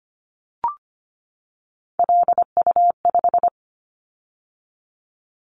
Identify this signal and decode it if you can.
Morse code, heard as a keyed tone, about 25 wpm, 700 Hz. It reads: LU5
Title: Callsign Prefixes for Top 100 DXCC - 25wpm